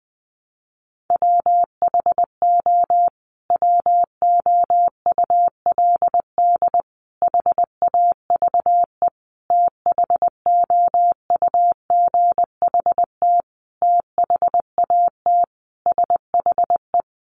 Morse code, heard as a keyed tone, about 20 wpm, 700 Hz. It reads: WHO WOULD HAVE THOUGHT THAT SHE
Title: Sentences from Top 1200 Words 20wpm